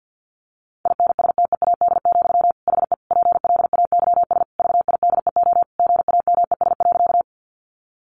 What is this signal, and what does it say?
Morse code, heard as a keyed tone, about 48 wpm, 700 Hz. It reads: INSTEAD, HE PLAYS VIDEO GAMES.
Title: Additional Easy Phrases 50wpm